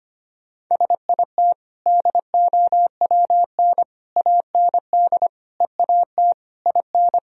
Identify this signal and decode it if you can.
Morse code, heard as a keyed tone, about 25 wpm, 700 Hz. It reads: SIT DOWN AND EAT IN
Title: Sentences from Top 1000 Words - Review Part 4 - 25wpm